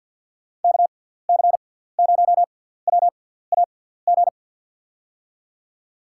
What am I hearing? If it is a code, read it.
Morse code, heard as a keyed tone, about 49 wpm, spaced as for 17 wpm, 700 Hz. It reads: KC0WAG
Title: USA Call Signs - 6x Character Spacing - 50wpm